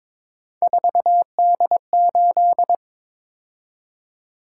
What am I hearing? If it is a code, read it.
Morse code, heard as a keyed tone, about 22 wpm, 700 Hz. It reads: 4D8
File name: practice / 22wpm / Three Letter-Number Combinations - 22wpm